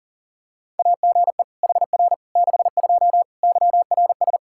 Morse code, heard as a keyed tone, about 40 wpm, 700 Hz. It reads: AGE HR 62 YRS
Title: Practice QSO - Exchange 3 - More Info - 40wpm